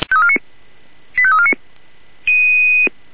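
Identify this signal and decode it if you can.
Five-tone selective call (ZVEI), heard as quick runs of five tones, tones of about 70 ms. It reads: 85349
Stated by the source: ZVEI five-tone selective call received on SDR.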